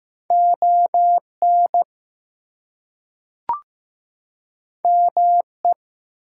Morse code, heard as a keyed tone, about 15 wpm, 700 Hz. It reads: ON ME
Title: Two Letter Words - 15wpm